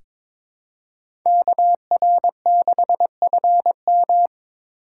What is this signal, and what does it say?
Morse code, heard as a keyed tone, about 22 wpm, 700 Hz. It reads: KR6FM
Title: USA Call Signs - 5 Characters - 3x Character Spacing - 22wpm